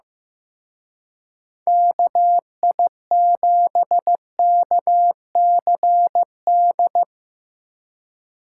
Morse code, heard as a keyed tone, about 16 wpm, 700 Hz. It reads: KI7KCD
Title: Speed Racing - USA 2x3 Call Signs 15wpm